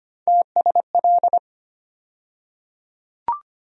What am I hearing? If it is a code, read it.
Morse code, heard as a keyed tone, about 25 wpm, 700 Hz. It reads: TSL
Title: USA Call Sign Postfixes - 3 Letters 25wpm